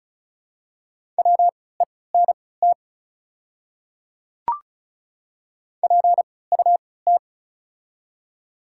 Morse code, heard as a keyed tone, about 36 wpm, spaced as for 20 wpm, 700 Hz. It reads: WENT PUT
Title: 3x Character Spacing - Top 200 Words - 35wpm